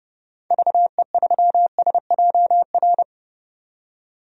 Morse code, heard as a keyed tone, about 32 wpm, 700 Hz. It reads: VE3SJR
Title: DXCC-100 Call Signs - 6 Characters - 30wpm